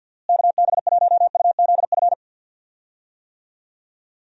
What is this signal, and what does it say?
Morse code, heard as a keyed tone, about 51 wpm, 700 Hz. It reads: KB1UZF
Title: USA Call Signs - 6 Characters - Repeated 2x - 50wpm